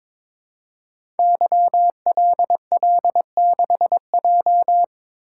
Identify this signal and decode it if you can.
Morse code, heard as a keyed tone, about 22 wpm, 700 Hz. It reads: YLL6J
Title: Code Groups - 5 Letter:Numbers - 22wpm